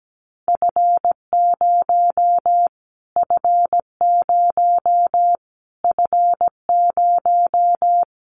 Morse code, heard as a keyed tone, about 17 wpm, 700 Hz. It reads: F0 F0 F0
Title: Callsign Prefixes for Top 25 DXCC - Repeated 3x - 17wpm